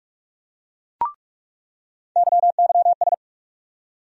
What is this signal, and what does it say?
Morse code, heard as a keyed tone, about 42 wpm, 700 Hz. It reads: YYS
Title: USA Call Sign Postfixes - 3 Letters 45wpm